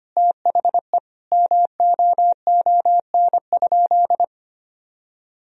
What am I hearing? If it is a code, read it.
Morse code, heard as a keyed tone, about 25 wpm, 700 Hz. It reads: THE MOON?
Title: Sentences from Top 1000 Words - Review Part 3 - 25wpm